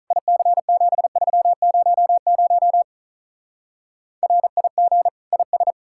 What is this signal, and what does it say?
Morse code, heard as a keyed tone, about 42 wpm, 700 Hz. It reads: IC7300 RIG IS
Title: Speed-Racing Common QSO Phrases Encore - 35wpm